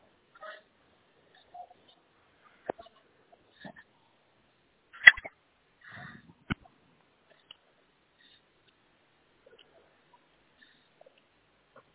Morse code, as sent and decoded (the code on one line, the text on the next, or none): none